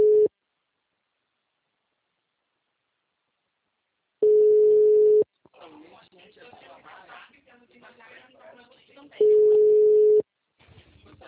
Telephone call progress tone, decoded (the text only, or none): ringback tone